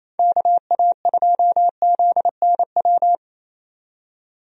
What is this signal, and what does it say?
Morse code, heard as a keyed tone, about 28 wpm, 700 Hz. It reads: KA2ZNW
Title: USA Call Signs - 6 Characters - 28wpm